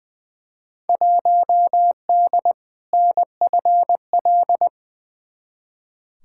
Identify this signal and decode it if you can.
Morse code, heard as a keyed tone, about 20 wpm, 700 Hz. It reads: 1D NFL
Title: Field Day Exchanges - 20wpm